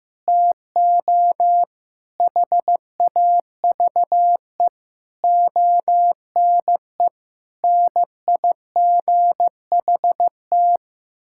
Morse code, heard as a keyed tone, about 15 wpm, 700 Hz. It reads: TO HAVE ONE NIGHT
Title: Sentences from Top 1600 Words 15wpm